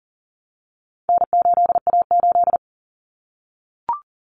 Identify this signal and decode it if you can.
Morse code, heard as a keyed tone, about 40 wpm, 700 Hz. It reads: N8A8
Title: Common Cut Numbers - 4-Digit 40wpm